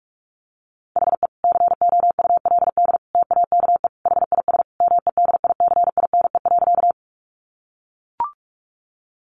Morse code, heard as a keyed tone, about 44 wpm, 700 Hz. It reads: HE COULD TAKE HIS MEDICINE.
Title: Sentences from Top 900 Words 45wpm